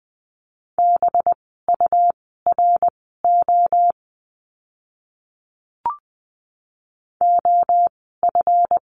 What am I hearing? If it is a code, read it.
Morse code, heard as a keyed tone, about 20 wpm, spaced as for 14 wpm, 700 Hz. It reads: BURO OF
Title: 185 Lesson 30f - Letter b - Words 20wpm